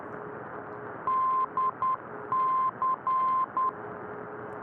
Morse code, heard as a keyed tone, about 10 wpm, 1040 Hz, with radio noise. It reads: DC